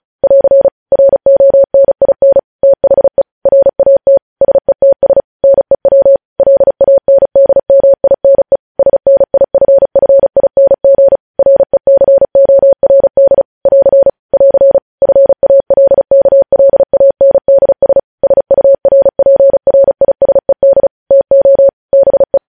Morse code, heard as a keyed tone, about 36 wpm, 550 Hz. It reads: <AR> RONIN THE RAT SETS NEW LANDMINE SNIFFING RECORD <AR> <AR> FALKLANDS SURPRISED TO BE
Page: news headlines in morse code at 35 wpm